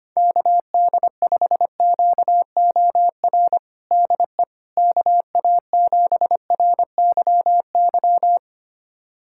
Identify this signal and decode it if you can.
Morse code, heard as a keyed tone, about 25 wpm, 700 Hz. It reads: KD5QOR DE KA7RYY
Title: USA Callsign DE Callsign - 6 Characters - 25wpm